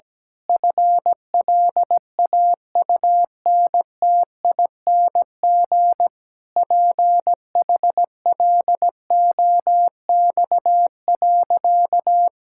Morse code, heard as a keyed tone, about 17 wpm, 700 Hz. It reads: FLAUNTING PHLOX.
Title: Pangram Challenge v2 - 17wpm